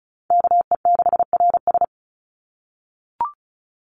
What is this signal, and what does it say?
Morse code, heard as a keyed tone, about 34 wpm, 700 Hz. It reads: KE6RS
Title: NA SOTA Chaser Callsigns 35wpm